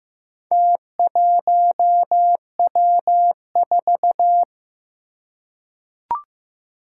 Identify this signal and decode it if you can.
Morse code, heard as a keyed tone, about 15 wpm, 700 Hz. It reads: T1W4
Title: Four Challenging Letter-Number Combinations 15wpm